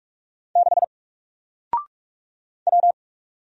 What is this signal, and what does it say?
Morse code, heard as a keyed tone, about 46 wpm, 700 Hz. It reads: 6 W
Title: Single Letter-Number Alt 45wpm